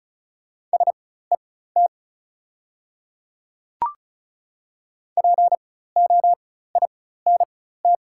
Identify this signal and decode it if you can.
Morse code, heard as a keyed tone, about 36 wpm, spaced as for 16 wpm, 690 Hz. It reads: SET POINT
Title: 4x Character Spacing - Top 300 Words - 35wpm